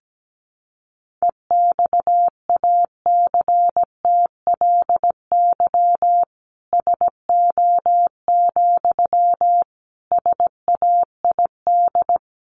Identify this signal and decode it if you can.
Morse code, heard as a keyed tone, about 17 wpm, 700 Hz. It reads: EXACTLY SO, SAID
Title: Sentences from Top 900 Words 17wpm